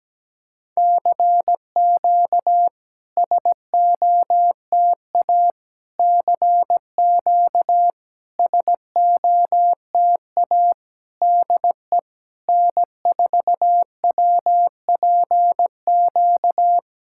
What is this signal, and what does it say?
Morse code, heard as a keyed tone, about 17 wpm, 700 Hz. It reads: CQ SOTA CQ SOTA DE N4WPQ
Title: SOTA QSO Practice 17wpm